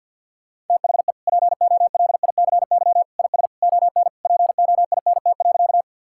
Morse code, heard as a keyed tone, about 51 wpm, 690 Hz. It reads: THE POLICY IS ON POINT.
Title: Sentences from Top 1000 Words - Review Part 5 - 50wpm